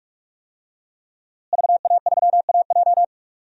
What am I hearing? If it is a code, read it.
Morse code, heard as a keyed tone, about 44 wpm, 700 Hz. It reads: VA3AJ
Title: DXCC-100 Call Signs - 5 Characters - Repeated 3x - 45wpm